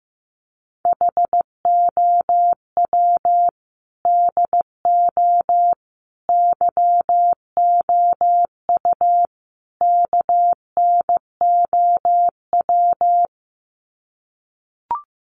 Morse code, heard as a keyed tone, about 15 wpm, 700 Hz. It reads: HOW DO YOU KNOW
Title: Sets of 4 Words 15wpm